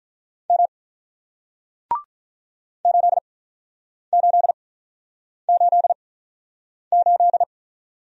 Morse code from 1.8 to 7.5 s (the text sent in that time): 8888